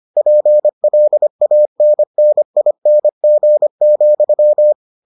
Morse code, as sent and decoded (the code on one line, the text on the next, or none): .--. .-.. .- -. -. .. -. --. --..--
PLANNING,